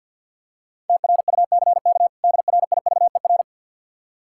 Morse code, heard as a keyed tone, about 52 wpm, 700 Hz. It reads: TRUCK DRIVER
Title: Practice QSO - Exchange 3 - More Info - 50wpm